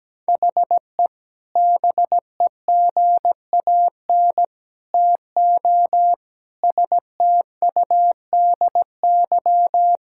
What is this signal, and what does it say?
Morse code, heard as a keyed tone, about 17 wpm, 700 Hz. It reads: HE BEGAN TO STUDY
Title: Sentences from Top 1000 Words - Review Part 6 - 17wpm